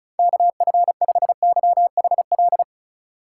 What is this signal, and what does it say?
Morse code, heard as a keyed tone, about 36 wpm, 700 Hz. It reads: KF5YHL
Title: USA Call Signs - 6 Characters - 4x Character Spacing - 35wpm